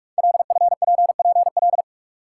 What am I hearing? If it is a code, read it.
Morse code, heard as a keyed tone, about 44 wpm, 690 Hz. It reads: LFPPL
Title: Code Groups - 5 Letters - 45wpm